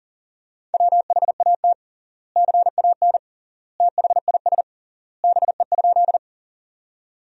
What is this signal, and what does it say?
Morse code, heard as a keyed tone, about 40 wpm, 700 Hz. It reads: WHAT CAN THIS BE?
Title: 3x Word-Spacing - Sentences from Top 100 Words - 40wpm